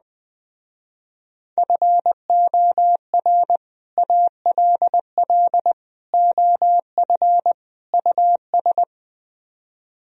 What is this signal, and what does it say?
Morse code, heard as a keyed tone, about 21 wpm, 700 Hz. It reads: FOR ALL OF US